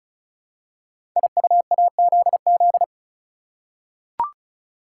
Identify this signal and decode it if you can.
Morse code, heard as a keyed tone, about 36 wpm, 690 Hz. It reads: IUAZZ
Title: Code Groups - 5 Letters - 35wpm